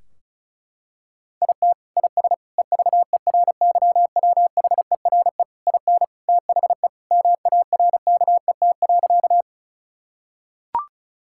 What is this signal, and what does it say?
Morse code, heard as a keyed tone, about 36 wpm, 700 Hz. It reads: IT IS EVERYWHERE IN THE MARKET.
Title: Sentences from Top 1600 Words 35wpm